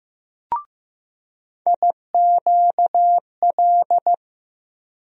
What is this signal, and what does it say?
Morse code, heard as a keyed tone, about 15 wpm, 700 Hz. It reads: IQL
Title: USA Call Sign Postfixes - 3 Letters 15wpm